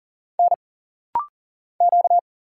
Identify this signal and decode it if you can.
Morse code, heard as a keyed tone, about 40 wpm, 700 Hz. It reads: N Q
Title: Single Letter-Number Alt - Rapid-Fire 40wpm